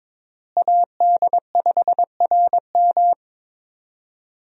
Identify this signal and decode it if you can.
Morse code, heard as a keyed tone, about 21 wpm, 700 Hz. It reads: AD5RM
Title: USA Call Signs - 5 Characters - Repeated 2x - 22wpm